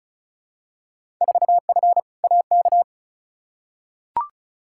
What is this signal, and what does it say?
Morse code, heard as a keyed tone, about 34 wpm, 700 Hz. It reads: 4F AK
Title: Field Day Exchanges Encore - 35wpm